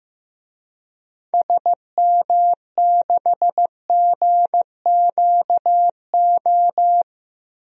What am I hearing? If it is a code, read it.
Morse code, heard as a keyed tone, about 15 wpm, 700 Hz. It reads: SM6GQO
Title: DXCC-100 Call Signs - 6 Characters - Repeated 2x - 15wpm